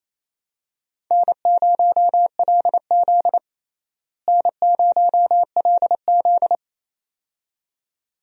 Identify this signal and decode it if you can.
Morse code, heard as a keyed tone, about 28 wpm, 700 Hz. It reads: N0LZ N0LZ
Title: USA Call Signs - 4 Characters - Repeated 2x - 28wpm